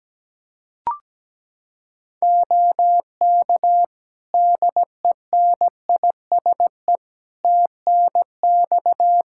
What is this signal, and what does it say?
Morse code, heard as a keyed tone, about 17 wpm, 700 Hz. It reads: OK DENISE TNX
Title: K1USN SST QSO Practice 17wpm